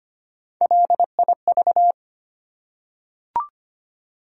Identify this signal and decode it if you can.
Morse code, heard as a keyed tone, about 25 wpm, 700 Hz. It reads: LIV
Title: USA Common Names - Max 3 Letters - 25wpm